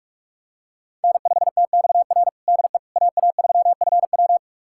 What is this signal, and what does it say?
Morse code, heard as a keyed tone, about 46 wpm, 700 Hz. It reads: N5TXR DE AA3FW
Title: USA Callsign DE Callsign - 5 Characters - 45wpm